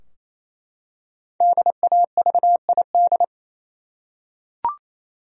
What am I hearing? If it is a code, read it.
Morse code, heard as a keyed tone, about 27 wpm, 700 Hz. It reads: DAVID
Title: USA Common Names - 28wpm